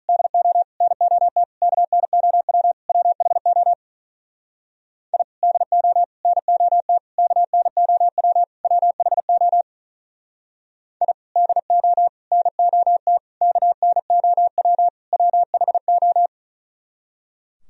Morse code, heard as a keyed tone, about 43 wpm, 700 Hz. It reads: DO NOT KNOW WHO I DO NOT KNOW WHO I DO NOT KNOW WHO
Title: Speed-Racing - Sets of 5 Words 35wpm